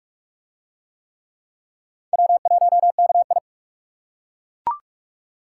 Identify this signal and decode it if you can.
Morse code, heard as a keyed tone, about 38 wpm, 700 Hz. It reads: W1KI